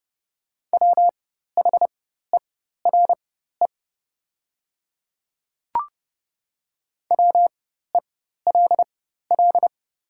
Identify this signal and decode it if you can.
Morse code, heard as a keyed tone, about 30 wpm, spaced as for 14 wpm, 700 Hz. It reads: WHERE WELL
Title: 4x Character Spacing - Top 200 Words - 30wpm